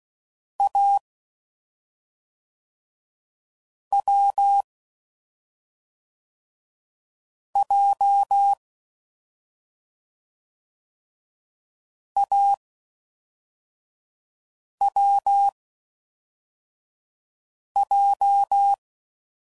Morse code, heard as a keyed tone, about 16 wpm, 800 Hz. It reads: A W J A W J